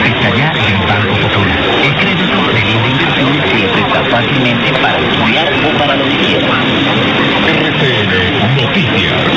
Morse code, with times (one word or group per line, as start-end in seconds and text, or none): none